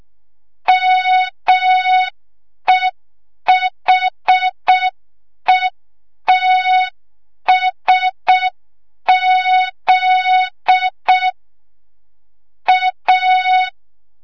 Morse code, heard as a keyed tone, about 6 wpm, 3700 Hz. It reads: MEHETSZ A
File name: Morze-HA2MN / hang / 25 GYAKORLAT